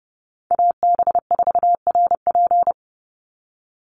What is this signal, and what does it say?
Morse code, heard as a keyed tone, about 30 wpm, 700 Hz. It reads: AB4RP
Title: USA Call Signs 30wpm